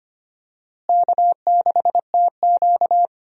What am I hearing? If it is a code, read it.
Morse code, heard as a keyed tone, about 25 wpm, 700 Hz. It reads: K6TQ
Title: USA Call Signs - 4 Characters - Repeated 3x - 25wpm